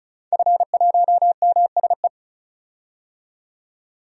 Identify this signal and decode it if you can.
Morse code, heard as a keyed tone, about 36 wpm, 700 Hz. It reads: F1MSE
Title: DXCC-100 Call Signs - 5 Characters - 35wpm